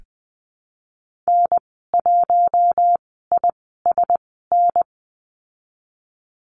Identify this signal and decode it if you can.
Morse code, heard as a keyed tone, about 20 wpm, spaced as for 15 wpm, 700 Hz. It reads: N1ISN
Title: 055 Lesson 10g - Letter r - Call Signs 20wpm